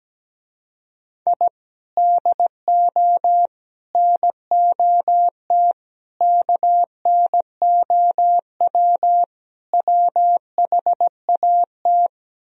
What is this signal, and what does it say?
Morse code, heard as a keyed tone, about 17 wpm, 700 Hz. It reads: I DO NOT KNOW WHAT
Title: Speed-Racing - Sets of 5 Words 17wpm